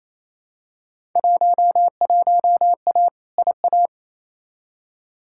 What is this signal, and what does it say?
Morse code, heard as a keyed tone, about 28 wpm, 700 Hz. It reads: 11A IA
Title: Field Day Exchanges Encore - 28wpm